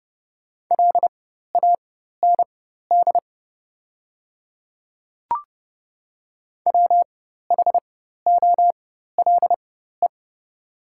Morse code, heard as a keyed tone, about 30 wpm, spaced as for 14 wpm, 700 Hz. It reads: LAND WHOLE
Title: 4x Character Spacing - Top 500 Words - 30wpm